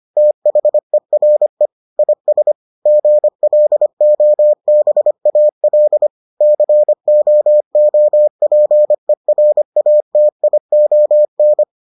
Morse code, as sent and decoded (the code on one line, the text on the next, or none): - .... .-. / .. ... / --. .-.. --- -... .- .-.. / -.-. --- --- .--. .-. .- - .. --- -.
THERE IS GLOBAL COOPERATION